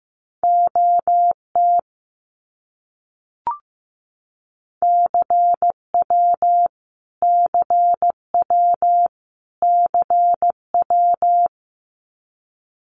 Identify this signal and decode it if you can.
Morse code, heard as a keyed tone, about 15 wpm, 700 Hz. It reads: OT CW CW CW
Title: Abbreviations Repeated 3x - 15wpm